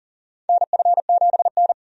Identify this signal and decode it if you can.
Morse code, heard as a keyed tone, about 40 wpm, 700 Hz. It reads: NF7N
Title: USA Call Signs - 4 Characters - 40wpm